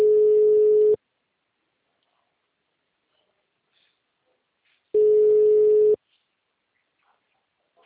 Ringback tone (Europe, 425 Hz).